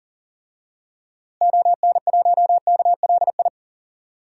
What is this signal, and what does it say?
Morse code, heard as a keyed tone, about 40 wpm, 700 Hz. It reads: ON1KLI
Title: DXCC-100 Call Signs - 6 Characters - Repeated 3x - 40wpm